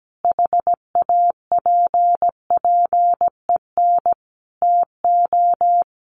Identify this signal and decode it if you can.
Morse code, heard as a keyed tone, about 17 wpm, 700 Hz. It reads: HAPPEN TO
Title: Sentences from Top 1400 Words 17wpm